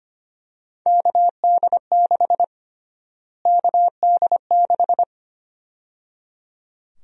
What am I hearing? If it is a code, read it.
Morse code, heard as a keyed tone, about 25 wpm, 700 Hz. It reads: KD6 KD6
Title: USA Call Sign Prefixes - Repeated 2x - 25wpm